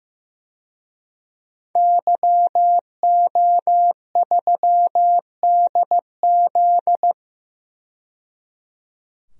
Morse code, heard as a keyed tone, about 15 wpm, 700 Hz. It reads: YO3DZ
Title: DXCC-100 Call Signs - 5 Characters - 15wpm